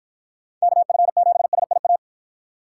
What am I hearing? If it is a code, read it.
Morse code, heard as a keyed tone, about 54 wpm, 700 Hz. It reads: KF7SIA